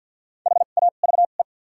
Morse code, heard as a keyed tone, about 51 wpm, spaced as for 36 wpm, 700 Hz. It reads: HAVE
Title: 2x Character Spacing - Top 100 Words - 50wpm